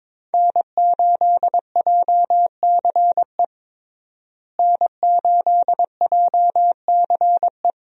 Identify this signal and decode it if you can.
Morse code, heard as a keyed tone, about 22 wpm, 690 Hz. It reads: N8JCE N8JCE
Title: USA Call Signs - 5 Characters - Repeated 3x - 22wpm